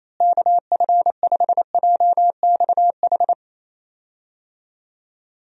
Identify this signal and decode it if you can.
Morse code, heard as a keyed tone, about 28 wpm, 700 Hz. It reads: KF5JXH
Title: USA Call Signs - 6 Characters - Repeated 2x - 28wpm